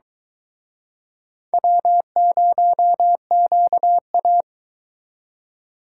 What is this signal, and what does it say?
Morse code, heard as a keyed tone, about 25 wpm, 700 Hz. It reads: W0QA